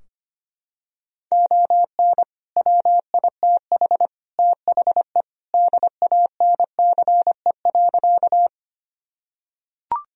Morse code, heard as a keyed tone, about 25 wpm, 700 Hz. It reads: ON WITH THE DANCE.
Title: Sentences from Top 1000 Words - Review Part 4 - 25wpm